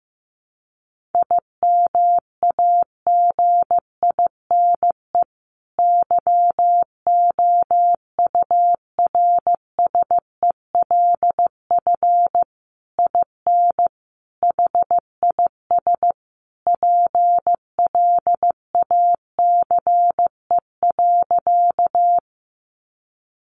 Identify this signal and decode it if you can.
Morse code, heard as a keyed tone, about 15 wpm, 700 Hz. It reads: IMAGINE YOURSELF IN HIS PLACE.
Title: Sentences from Top 1400 Words 15wpm